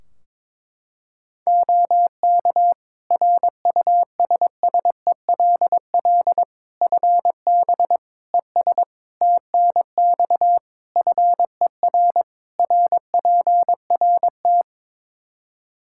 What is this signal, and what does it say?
Morse code, heard as a keyed tone, about 22 wpm, 700 Hz. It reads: OK RUSSELL FB ES TNX FER RPRT
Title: Practice QSO - Exchange 2 - Info - 22wpm